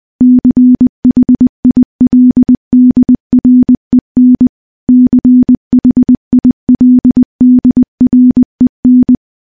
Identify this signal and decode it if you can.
Morse code, heard as a keyed tone, about 20 wpm, 250 Hz. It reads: CHILDREN CHILDREN
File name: morse-training-20WPM